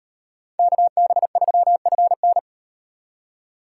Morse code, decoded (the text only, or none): KB3FN